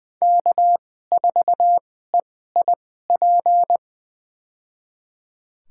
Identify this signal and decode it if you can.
Morse code, heard as a keyed tone, about 20 wpm, spaced as for 14 wpm, 700 Hz. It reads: K4EIP
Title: USA Call Signs - 2x Character Spacing - 20wpm